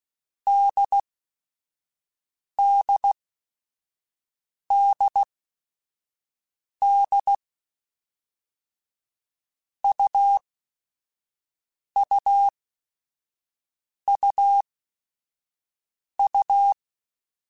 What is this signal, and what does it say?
Morse code, heard as a keyed tone, about 16 wpm, spaced as for 5 wpm, 800 Hz. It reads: DDDD UUUU